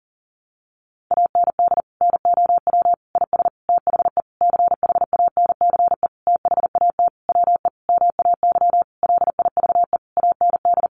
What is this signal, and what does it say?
Morse code, heard as a keyed tone, about 40 wpm, 700 Hz. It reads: AND NOW IS THE CHANCE THAT WE MAY LIVE AND
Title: Sentences from Top 800 Words 40wpm